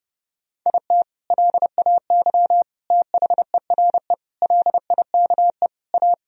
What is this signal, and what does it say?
Morse code, heard as a keyed tone, about 30 wpm, 700 Hz. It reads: IT LAY THERE LIKE A
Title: Sentences from Top 1000 Words - Review Part 3 - 30wpm